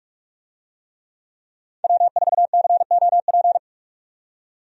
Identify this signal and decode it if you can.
Morse code, heard as a keyed tone, about 43 wpm, 700 Hz. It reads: W4COP
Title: USA Call Signs - 5 Characters - 45wpm